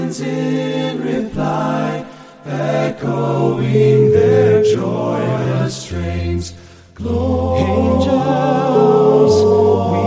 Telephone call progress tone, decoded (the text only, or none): ringback tone